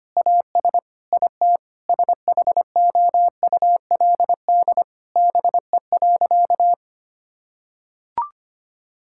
Morse code, heard as a keyed tone, about 25 wpm, 700 Hz. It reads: AS IT SHOULD BE.
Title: Sentences from Top 1000 Words - Review Part 6 - 25wpm